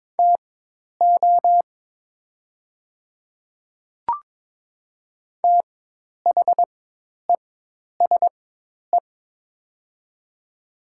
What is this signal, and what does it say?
Morse code, heard as a keyed tone, about 22 wpm, spaced as for 10 wpm, 700 Hz. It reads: TO THESE
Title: 4x Character Spacing - Top 100 Words - 22wpm